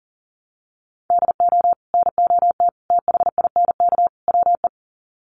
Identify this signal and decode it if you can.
Morse code, heard as a keyed tone, about 40 wpm, 700 Hz. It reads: DO NOT THINK WE
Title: Sets of 4 Words 40wpm